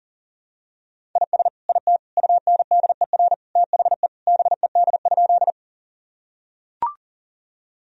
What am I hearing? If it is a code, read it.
Morse code, heard as a keyed tone, about 40 wpm, 700 Hz. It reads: IS IT UNDER THE BED?
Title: Sentences from Top 1000 Words - Review Part 1 - 40wpm